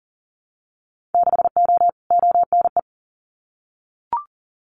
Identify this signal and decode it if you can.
Morse code, heard as a keyed tone, about 41 wpm, 700 Hz. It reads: 6O ONE
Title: Winter Field Day Exchanges Encore 40wpm